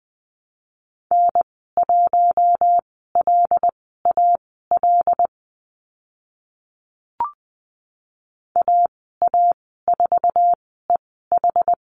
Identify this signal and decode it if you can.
Morse code, heard as a keyed tone, about 20 wpm, spaced as for 15 wpm, 700 Hz. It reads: N1LAL AA4EH
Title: 088 Lesson 15g - Number 5 - Call Signs 20wpm